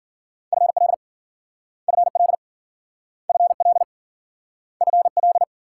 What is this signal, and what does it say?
Morse code, heard as a keyed tone, about 53 wpm, 700 Hz. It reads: FL FL FL FL